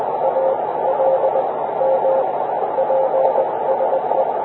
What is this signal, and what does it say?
Morse code, heard as a keyed tone, about 20 wpm, spaced as for 16 wpm, 530 Hz, with radio noise. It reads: ALM PSE